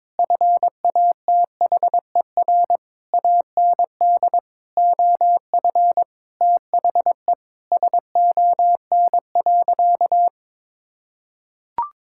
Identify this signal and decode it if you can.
Morse code, heard as a keyed tone, about 21 wpm, 700 Hz. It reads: FATHER AND OF THE SON.
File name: Sentences with 9-Word Phrases - 22wpm